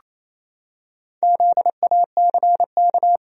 Morse code, heard as a keyed tone, about 28 wpm, 700 Hz. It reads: ZACK
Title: USA Common Names - Max 5 Letters - 28wpm